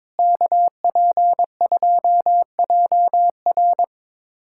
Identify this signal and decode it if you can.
Morse code, heard as a keyed tone, about 22 wpm, 700 Hz. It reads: KP2JR
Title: USA Callsign DE Callsign - 5 Characters - 22wpm